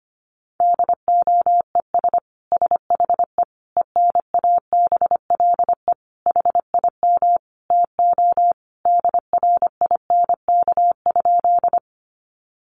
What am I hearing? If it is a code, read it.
Morse code, heard as a keyed tone, about 25 wpm, 700 Hz. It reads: DOES SHE ENABLE HIM TO DRINK?
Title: Sentences from Top 1500 Words 25wpm